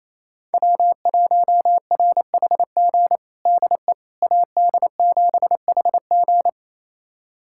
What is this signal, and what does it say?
Morse code, heard as a keyed tone, about 28 wpm, 700 Hz. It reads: W1RHG DE AD7HG
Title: USA Callsign DE Callsign - 5 Characters - 28wpm